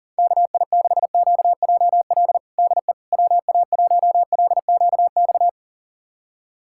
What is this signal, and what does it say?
Morse code, heard as a keyed tone, about 40 wpm, 690 Hz. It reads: KI6QJL DE WA1LQX
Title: USA Callsign DE Callsign - 6 Characters - 40wpm